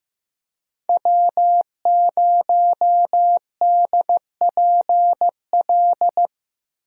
Morse code, heard as a keyed tone, about 15 wpm, 700 Hz. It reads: W0DPL
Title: USA Call Signs - 5 Characters - 15wpm